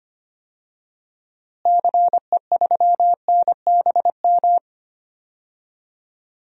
Morse code, heard as a keyed tone, about 25 wpm, 700 Hz. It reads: CE3NBM
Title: DXCC-100 Call Signs - 6 Characters - Repeated 3x - 25wpm